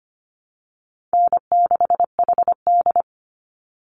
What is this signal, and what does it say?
Morse code, heard as a keyed tone, about 25 wpm, 700 Hz. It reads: N6HD
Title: Easy Call Signs Repeated 3x - 25wpm